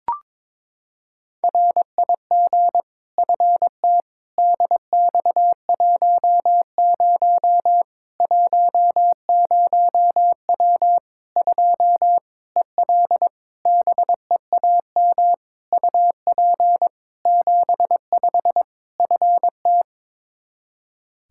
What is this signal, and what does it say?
Morse code, heard as a keyed tone, about 22 wpm, 700 Hz. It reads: RIG FT DX10 10W 2 EL BEAM UP 75 FT
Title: K1USN SST QSO Practice 22wpm